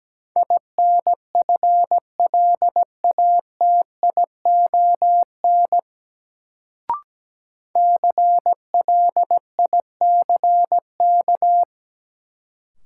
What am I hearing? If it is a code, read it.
Morse code, heard as a keyed tone, about 17 wpm, 700 Hz. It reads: INFLATION CLICK
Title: Sentences from Top 1100 Words - New Word Drill - 17wpm